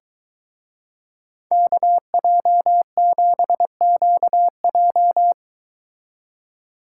Morse code, heard as a keyed tone, about 23 wpm, 700 Hz. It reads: KJ7QJ